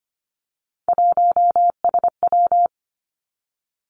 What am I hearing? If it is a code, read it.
Morse code, heard as a keyed tone, about 25 wpm, 700 Hz. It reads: 1SW
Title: Three Challenging Letter-Number Combinations 25wpm